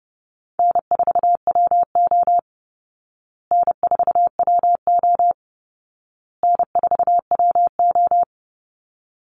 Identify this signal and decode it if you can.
Morse code, heard as a keyed tone, about 30 wpm, 700 Hz. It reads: N4WO N4WO N4WO
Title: Easy Call Signs Repeated 3x - 30wpm